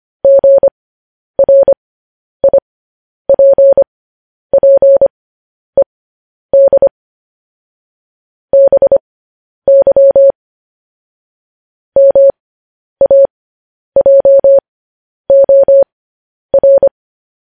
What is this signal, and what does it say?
Morse code, heard as a keyed tone, about 25 wpm, spaced as for 10 wpm, 550 Hz. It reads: GRIPPED BY MAJOR
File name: podcast / News headlines-1x-25x10